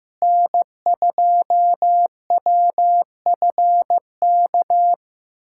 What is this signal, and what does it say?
Morse code, heard as a keyed tone, about 15 wpm, 700 Hz. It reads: N2WFK
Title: USA Callsign DE Callsign 15wpm